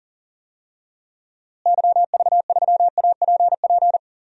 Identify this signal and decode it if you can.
Morse code, heard as a keyed tone, about 40 wpm, 700 Hz. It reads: YV3APP
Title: DXCC-100 Call Signs - 6 Characters - Repeated 2x - 40wpm